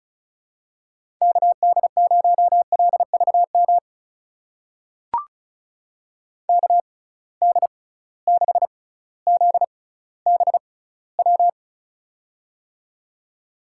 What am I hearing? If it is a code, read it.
Morse code, heard as a keyed tone, about 36 wpm, spaced as for 18 wpm, 700 Hz. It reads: KD0LVM KD6ZBW
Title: USA Call Signs - 6 Characters - 6x Character Spacing - 35wpm